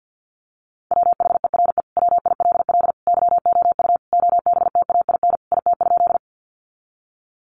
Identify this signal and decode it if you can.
Morse code, heard as a keyed tone, about 50 wpm, 700 Hz. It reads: WHERE WILL YOU OBTAIN IT?
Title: Sentences from Top 1300 Words 50wpm